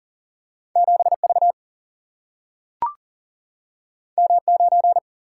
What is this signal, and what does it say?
Morse code, heard as a keyed tone, about 39 wpm, 700 Hz. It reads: 7V M9
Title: Single-Letter Single-Number 40wpm